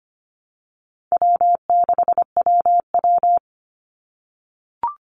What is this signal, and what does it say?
Morse code, heard as a keyed tone, about 25 wpm, 700 Hz. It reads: W6WW
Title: NA SOTA Chaser Callsigns 25wpm